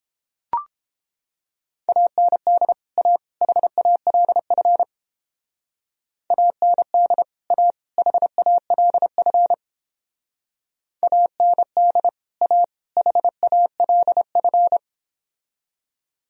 Morse code, heard as a keyed tone, about 31 wpm, 700 Hz. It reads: AND A HALF AND A HALF AND A HALF